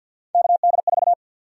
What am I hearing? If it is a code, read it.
Morse code, heard as a keyed tone, about 52 wpm, 700 Hz. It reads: KD4
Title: USA Call Sign Prefixes - Repeated 2x - 50wpm